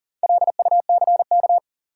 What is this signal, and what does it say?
Morse code, heard as a keyed tone, about 40 wpm, 700 Hz. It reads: LUCK
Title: Sentences from Top 1500 Words - New Word Drill - 40wpm